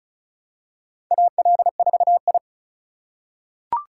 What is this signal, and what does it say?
Morse code, heard as a keyed tone, about 34 wpm, 700 Hz. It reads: AL4I
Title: USA Call Signs - 6x Character Spacing - 35wpm